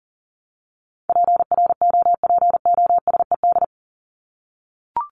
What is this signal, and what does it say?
Morse code, heard as a keyed tone, about 38 wpm, 700 Hz. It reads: PROPOSED
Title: ED Suffix - Top 100 Words with Base Words 40wpm